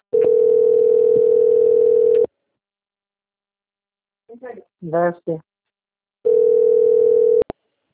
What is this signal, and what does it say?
Ringback tone (North America), 440 + 480 Hz, with speech.
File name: out- - - - - .23304